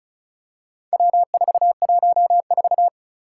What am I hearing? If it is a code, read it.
Morse code, heard as a keyed tone, about 36 wpm, 700 Hz. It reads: W414
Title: Four Challenging Letter-Number Combinations 35wpm